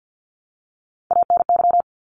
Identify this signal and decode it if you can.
Morse code, heard as a keyed tone, about 50 wpm, 700 Hz. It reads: ANY
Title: Top 200 Words Repeated 50wpm